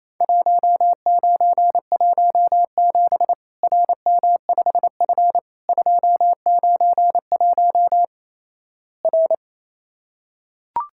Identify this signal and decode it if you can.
Morse code, heard as a keyed tone, about 28 wpm, 700 Hz. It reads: 1917 RM5F 291 R
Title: Worked All Europe DX Contest - QTC Exchange - 28wpm